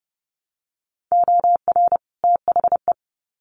Morse code, heard as a keyed tone, about 30 wpm, 700 Hz. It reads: OR THE
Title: Sets of 2 Words 30wpm